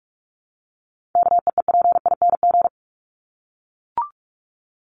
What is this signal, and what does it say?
Morse code, heard as a keyed tone, about 44 wpm, 700 Hz. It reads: KEEPING
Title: ING Suffix - Top 100 Words with Base Words 45wpm